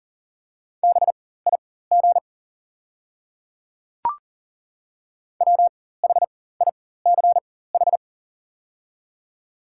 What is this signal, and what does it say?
Morse code, heard as a keyed tone, about 40 wpm, spaced as for 19 wpm, 700 Hz. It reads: BIG WHICH
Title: 4x Character Spacing - Top 200 Words - 40wpm